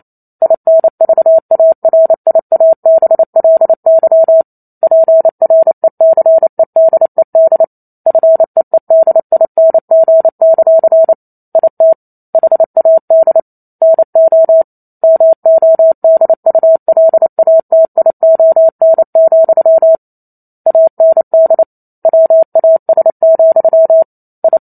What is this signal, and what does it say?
Morse code, heard as a keyed tone, about 28 wpm, 650 Hz. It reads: INVARIABLY PRECEDED FEEDING; IT HAD NO MODULATION, AND WAS, I
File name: Chapter 2 - 28 WPM